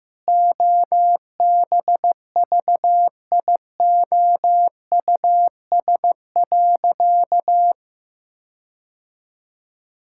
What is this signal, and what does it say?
Morse code, heard as a keyed tone, about 16 wpm, 700 Hz. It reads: OBVIOUS.
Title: Sentences from Top 1600 Words 15wpm